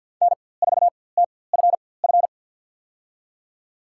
Morse code, heard as a keyed tone, about 52 wpm, spaced as for 23 wpm, 700 Hz. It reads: N4TFF
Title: USA Call Signs - 5 Characters - 4x Character Spacing - 50wpm